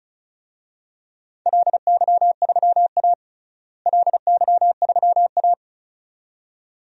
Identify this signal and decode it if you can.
Morse code, heard as a keyed tone, about 34 wpm, 700 Hz. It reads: LY3A LY3A
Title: DXCC-100 Call Signs - 4 Characters - Repeated 2x - 35wpm